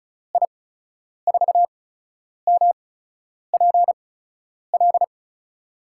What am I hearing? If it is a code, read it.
Morse code, heard as a keyed tone, about 36 wpm, spaced as for 10 wpm, 700 Hz. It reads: I4MPL
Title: USA Call Signs - 6 Characters - 8x Character Spacing - 35wpm